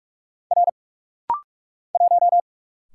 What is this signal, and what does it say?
Morse code, heard as a keyed tone, about 48 wpm, 700 Hz. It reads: R 1
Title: Single Letter-Number - Rapid-Fire 45wpm